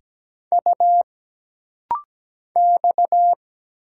Morse code, heard as a keyed tone, about 17 wpm, 700 Hz. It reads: U X
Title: Single Letters - Rapid-Fire 17wpm